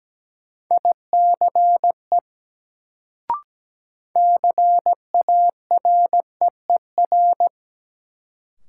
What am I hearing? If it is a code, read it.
Morse code, heard as a keyed tone, about 17 wpm, 700 Hz. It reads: ICE CAREER
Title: Sentences from Top 1000 Words - New Word Drill - 17wpm